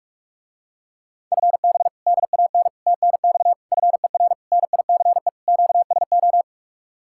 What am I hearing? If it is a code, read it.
Morse code, heard as a keyed tone, about 50 wpm, 700 Hz. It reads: FB DAN TNX FER NICE QSO